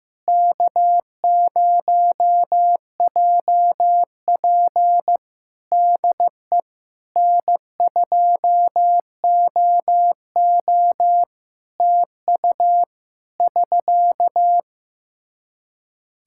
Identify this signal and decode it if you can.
Morse code, heard as a keyed tone, about 15 wpm, 700 Hz. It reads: K0JP DE N2OO TU <SK>
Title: Practice QSO - Exchange 4 - Ending - 15wpm